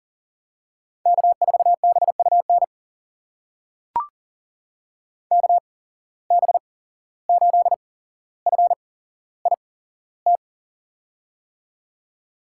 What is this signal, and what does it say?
Morse code, heard as a keyed tone, about 40 wpm, spaced as for 16 wpm, 700 Hz. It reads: K4BUN KB8FIT
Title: USA Call Signs - 8x Character Spacing - 40wpm